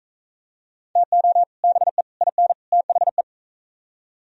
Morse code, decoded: TO BE IN THE

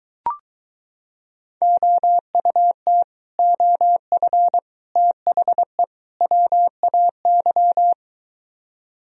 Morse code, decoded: OUT OF THE WAY

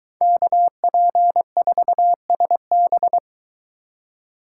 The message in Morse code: KP4SB

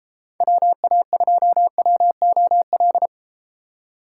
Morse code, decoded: WA2WOL

